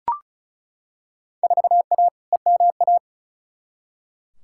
Morse code, heard as a keyed tone, about 36 wpm, 700 Hz. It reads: 4A EMA
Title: Field Day Exchanges - 35wpm